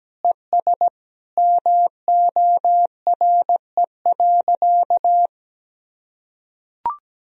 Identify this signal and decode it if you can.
Morse code, heard as a keyed tone, about 17 wpm, 700 Hz. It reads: ES MORE.
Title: Sentences from Top 500 Words and Abbreviations 17wpm